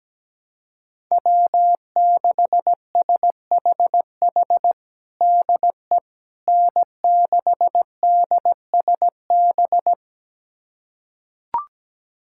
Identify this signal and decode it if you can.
Morse code, heard as a keyed tone, about 17 wpm, 700 Hz. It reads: W6SHH DE N6DSB
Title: USA Callsign DE Callsign - 5 Characters - 17wpm